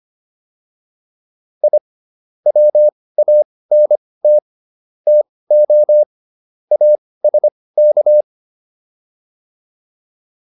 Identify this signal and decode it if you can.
Morse code, heard as a keyed tone, about 25 wpm, spaced as for 18 wpm, 600 Hz. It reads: I WANT TO ASK